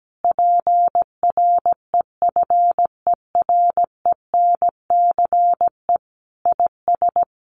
Morse code, heard as a keyed tone, about 17 wpm, 700 Hz. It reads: PREFERENCE IS
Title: Sentences from Top 1500 Words 17wpm